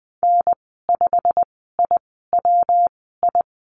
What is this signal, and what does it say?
Morse code, heard as a keyed tone, about 20 wpm, spaced as for 15 wpm, 700 Hz. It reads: N5IWI
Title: 147 Lesson 24g - Letter y - Call Signs 20wpm